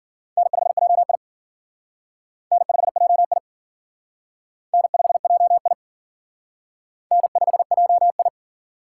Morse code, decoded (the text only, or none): N5JI N5JI N5JI N5JI